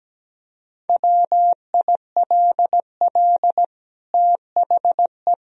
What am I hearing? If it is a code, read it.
Morse code, heard as a keyed tone, about 17 wpm, 700 Hz. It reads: WILL THE
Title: Sentences from Top 1000 Words - Review Part 3 - 17wpm